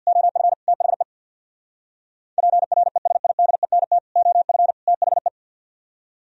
Morse code, heard as a keyed tone, about 58 wpm, 700 Hz. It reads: OF THE PRESIDENT OF THE
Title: Speed-Racing - Sets of 3 Words 50wpm